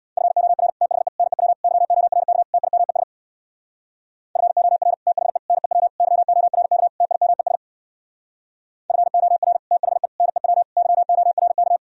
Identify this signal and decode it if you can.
Morse code, heard as a keyed tone, about 65 wpm, 700 Hz. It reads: FOR THE NEW YORK TIMES FOR THE NEW YORK TIMES FOR THE NEW YORK